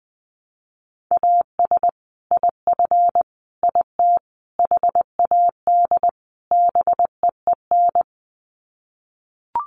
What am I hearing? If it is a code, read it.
Morse code, heard as a keyed tone, about 20 wpm, 700 Hz. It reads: AS IF IT HAD BEEN
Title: Sets of 5 Words Encore2 20wpm